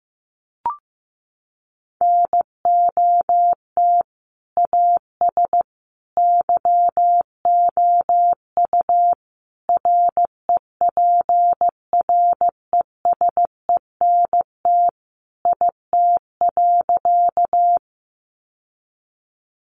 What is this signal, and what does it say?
Morse code, heard as a keyed tone, about 15 wpm, 700 Hz. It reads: NOT AS YOU REPRESENT IT.
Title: Sentences from Top 1200 Words 15wpm